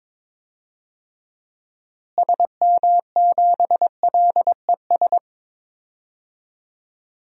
Morse code, heard as a keyed tone, about 21 wpm, 700 Hz. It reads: SM7LES
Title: DXCC-100 Call Signs - 6 Characters - Repeated 2x - 22wpm